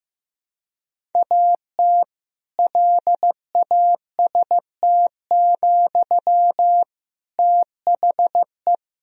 Morse code, heard as a keyed tone, about 15 wpm, 700 Hz. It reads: AT LAST, THE
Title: Sentences from Top 1000 Words - Review Part 3 - 15wpm